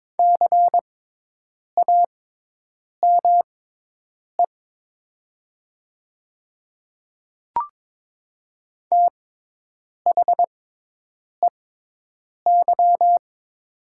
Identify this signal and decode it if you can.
Morse code, heard as a keyed tone, about 22 wpm, spaced as for 8 wpm, 700 Hz. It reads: CAME THEY
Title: 6x Character Spacing - Top 300 Words - 22wpm